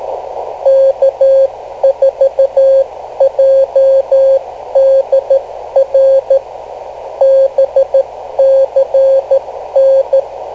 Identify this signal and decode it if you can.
Morse code, heard as a keyed tone, about 13 wpm, spaced as for 11 wpm, 540 Hz. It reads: K4JDR BCN
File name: kl7fdq / beacons / k4jdr-rst549